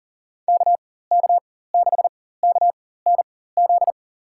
Morse code, heard as a keyed tone, about 37 wpm, spaced as for 18 wpm, 700 Hz. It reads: KK6KNZ
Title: USA Call Signs - 4x Character Spacing - 40wpm